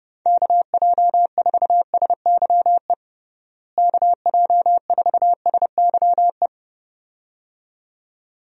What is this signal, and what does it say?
Morse code, heard as a keyed tone, about 32 wpm, 700 Hz. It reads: KJ4SYE KJ4SYE
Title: USA Call Signs - 6 Characters - Repeated 2x - 30wpm